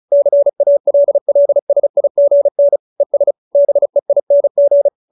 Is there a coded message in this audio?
Yes: Morse code from 0.1 to 4.9 s, heard as a keyed tone, about 34 wpm, 570 Hz. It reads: CALLSIGN ES BEING